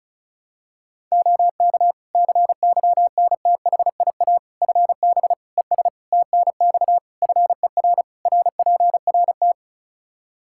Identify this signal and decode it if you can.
Morse code, heard as a keyed tone, about 36 wpm, 700 Hz. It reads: OK CYNTHIA FB ES TNX FER RPRT
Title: K1USN SST QSO Practice 35wpm